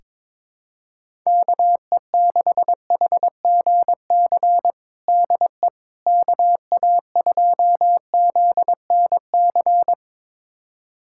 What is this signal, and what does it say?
Morse code, heard as a keyed tone, about 22 wpm, 700 Hz. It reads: KE6HGC DE KA2ZNC
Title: USA Callsign DE Callsign - 6 Characters - 22wpm